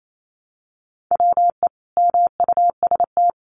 Morse code, heard as a keyed tone, about 28 wpm, 700 Hz. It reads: WE MUST
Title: Sentences from Top 1300 Words 28wpm